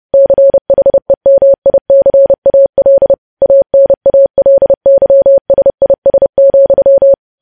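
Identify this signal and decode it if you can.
Morse code, heard as a keyed tone, about 30 wpm, 550 Hz. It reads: CHEMICAL ANALYSIS,
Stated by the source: News headlines sent in Morse code at 30 wpm.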